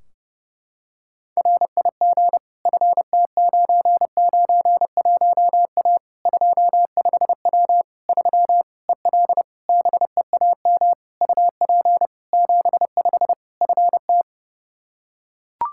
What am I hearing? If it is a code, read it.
Morse code, heard as a keyed tone, about 30 wpm, 700 Hz. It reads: RIG FT991A 25W 3 EL BEAM UP 75 FT
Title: K1USN SST QSO Practice 30wpm